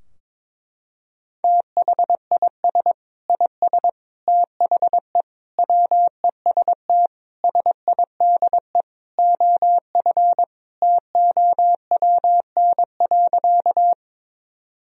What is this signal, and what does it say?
Morse code, heard as a keyed tone, about 22 wpm, 700 Hz. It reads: THIS IS THE WEST SIDE OF TOWN.
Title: Sentences from Top 1000 Words - Review Part 2 - 22wpm